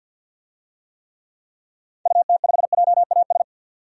Morse code, heard as a keyed tone, about 48 wpm, 700 Hz. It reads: UT5JAS
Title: DXCC-100 Call Signs - 6 Characters - 50wpm